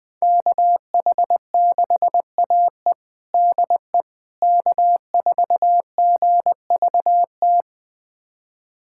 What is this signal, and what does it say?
Morse code, heard as a keyed tone, about 20 wpm, 700 Hz. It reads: KH6AE DE K4GVT
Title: USA Callsign DE Callsign - 5 Characters - 20wpm